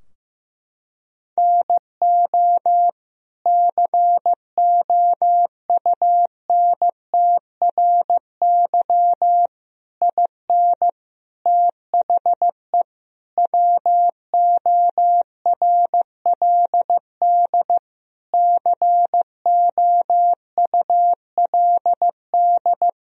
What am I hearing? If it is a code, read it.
Morse code, heard as a keyed tone, about 15 wpm, 700 Hz. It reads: NO COUNTRY IN THE WORLD COULD
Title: Sentences from Top 1000 Words - Review Part 3 - 15wpm